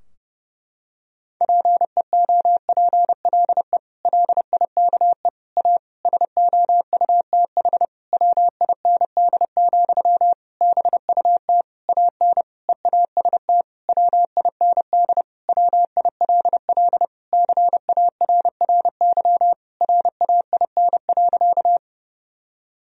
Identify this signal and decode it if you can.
Morse code, heard as a keyed tone, about 30 wpm, 700 Hz. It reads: PEOPLE LIKE A SOUTH WIND, BUT AN EAST WIND WILL CARRY RAIN.
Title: Sentences from Top 1000 Words - Review Part 3 - 30wpm